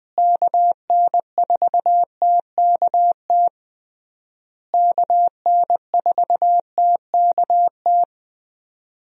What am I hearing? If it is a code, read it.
Morse code, heard as a keyed tone, about 20 wpm, 700 Hz. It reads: KN4TKT KN4TKT
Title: USA Call Signs - 6 Characters - Repeated 3x - 20wpm